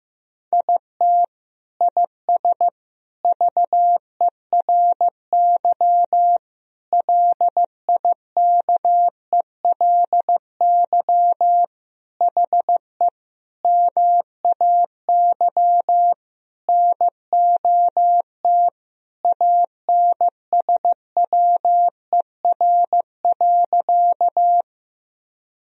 Morse code, heard as a keyed tone, about 15 wpm, 700 Hz. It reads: IT IS VERY LIKELY HE MAY NOT ANSWER.
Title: Sentences from Top 1000 Words - Review Part 4 - 15wpm